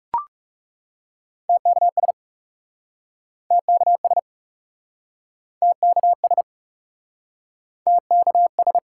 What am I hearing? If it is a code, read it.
Morse code, heard as a keyed tone, about 40 wpm, 700 Hz. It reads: TKS TKS TKS TKS